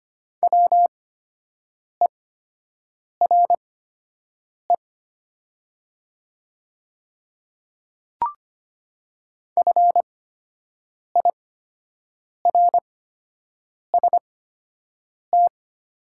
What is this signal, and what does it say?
Morse code, heard as a keyed tone, about 25 wpm, spaced as for 7 wpm, 700 Hz. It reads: WERE FIRST
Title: 8x Character Spacing - Top 100 Words - 25wpm